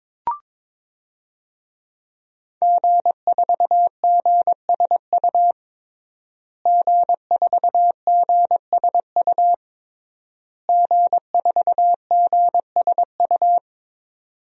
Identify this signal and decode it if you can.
Morse code, heard as a keyed tone, about 22 wpm, 700 Hz. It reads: G4GSU G4GSU G4GSU
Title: DXCC-100 Call Signs - 5 Characters - Repeated 3x - 22wpm